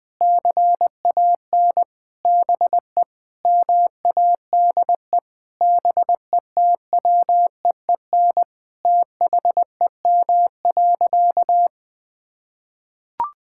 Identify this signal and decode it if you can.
Morse code, heard as a keyed tone, about 20 wpm, 700 Hz. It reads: CAN BE MADE BETWEEN THEM.
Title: Sentences from Top 1000 Words - Review Part 6 - 20wpm